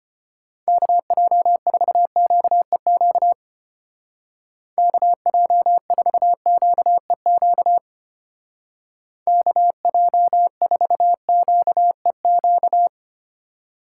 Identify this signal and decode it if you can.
Morse code, heard as a keyed tone, about 30 wpm, 700 Hz. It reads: KJ4QEQ KJ4QEQ KJ4QEQ
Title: Speed Racing - USA 2x3 Call Signs 25wpm